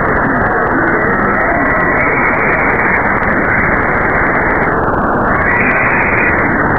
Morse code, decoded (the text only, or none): none